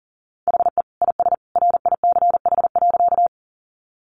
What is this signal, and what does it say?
Morse code, heard as a keyed tone, about 40 wpm, 700 Hz. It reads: HE IS RICH.
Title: Sentences from Top 1100 Words 40wpm